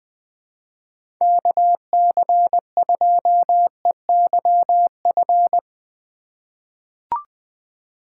Morse code, heard as a keyed tone, about 20 wpm, 700 Hz. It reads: KC2EYF
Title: USA Call Signs - 4x Character Spacing - 20wpm